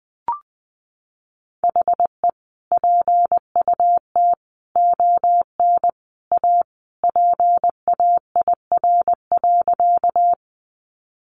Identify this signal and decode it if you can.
Morse code, heard as a keyed tone, about 20 wpm, 700 Hz. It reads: HE PUT ON A PAIR.
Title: Sentences from Top 1200 Words 20wpm